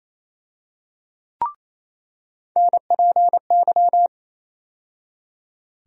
Morse code, heard as a keyed tone, about 28 wpm, 700 Hz. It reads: NPY
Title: USA Call Sign Postfixes - 3 Letters 28wpm